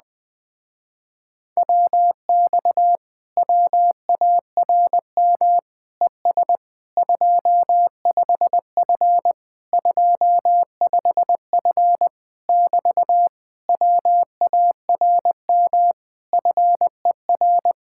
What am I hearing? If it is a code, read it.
Morse code, heard as a keyed tone, about 20 wpm, 700 Hz. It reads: WX WARM ES 25F 25F <BT> WARM FER
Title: Speed-Racing Common QSO Phrases Encore - 17wpm